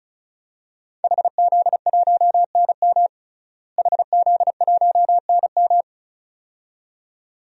Morse code, heard as a keyed tone, about 34 wpm, 700 Hz. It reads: HZ1NM HZ1NM
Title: DXCC-100 Call Signs - 5 Characters - Repeated 2x - 35wpm